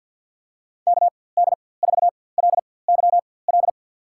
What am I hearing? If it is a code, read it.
Morse code, heard as a keyed tone, about 49 wpm, spaced as for 23 wpm, 700 Hz. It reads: KD4LYL
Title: USA Call Signs - 6 Characters - 4x Character Spacing - 50wpm